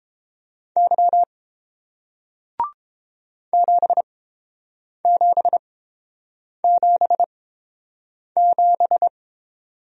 Morse code, heard as a keyed tone, about 29 wpm, spaced as for 7 wpm, 700 Hz. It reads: Y 7777